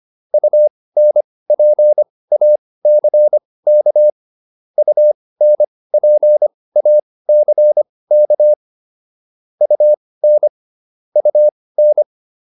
Morse code, heard as a keyed tone, about 25 wpm, spaced as for 18 wpm, 600 Hz. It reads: UNPACK UNPACK UN UN